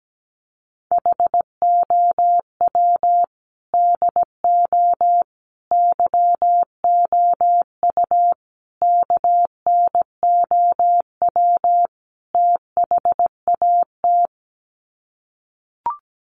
Morse code, heard as a keyed tone, about 17 wpm, 700 Hz. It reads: HOW DO YOU KNOW THAT
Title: Sets of 5 Words Encore 17wpm